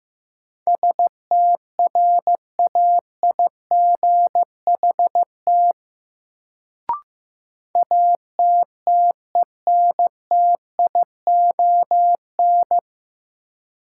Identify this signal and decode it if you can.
Morse code, heard as a keyed tone, about 15 wpm, 700 Hz. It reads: STRAIGHT ATTENTION
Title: Sentences from Top 1000 Words - New Word Drill - 15wpm